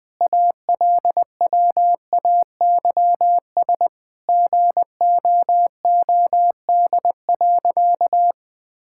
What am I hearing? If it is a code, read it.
Morse code, heard as a keyed tone, about 20 wpm, 700 Hz. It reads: ALWAYS GOOD.
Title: Sentences from Top 1000 Words - Review Part 5 - 20wpm